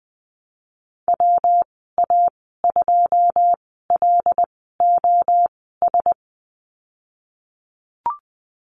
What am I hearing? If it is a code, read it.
Morse code, heard as a keyed tone, about 21 wpm, spaced as for 15 wpm, 700 Hz. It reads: WA2LOS
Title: 122 Lesson 20g - Number 3 - Call Signs 20wpm